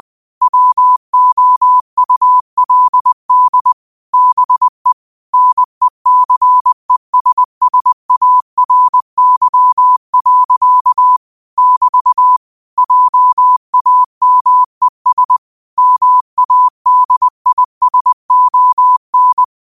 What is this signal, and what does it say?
Morse code, heard as a keyed tone, about 20 wpm, 1000 Hz. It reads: WOULD BE NECESSARY. <BT> JAMES MADISON
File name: qotdfiles / 20wpm